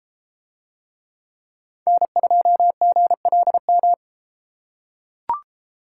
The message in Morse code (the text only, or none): N2GLM